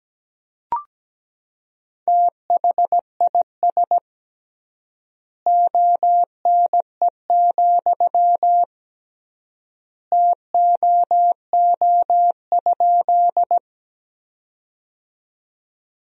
Morse code, heard as a keyed tone, about 17 wpm, 700 Hz. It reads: THIS ONE, TOO?
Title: 3x Word-Spacing - Sentences from Top 200 Words - 17wpm